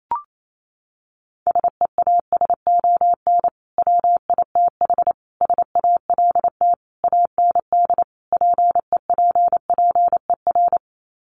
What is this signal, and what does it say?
Morse code, heard as a keyed tone, about 28 wpm, 700 Hz. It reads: SEASON WITH SALT AND PEPPER
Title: Sets of 5 Words 28wpm